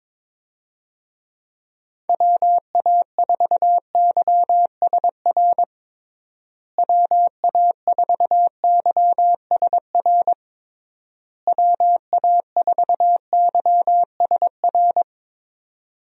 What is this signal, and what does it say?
Morse code, heard as a keyed tone, about 22 wpm, 700 Hz. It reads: WA4YSR WA4YSR WA4YSR
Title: USA Call Signs - 6 Characters - Repeated 3x - 22wpm